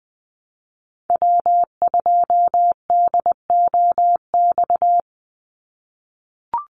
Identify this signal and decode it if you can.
Morse code, heard as a keyed tone, about 20 wpm, 700 Hz. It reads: W2DOX
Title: USA Call Signs 20wpm